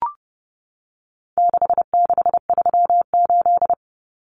Morse code, heard as a keyed tone, about 30 wpm, 700 Hz. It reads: 6638
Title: Numbers - 4-Digit 30wpm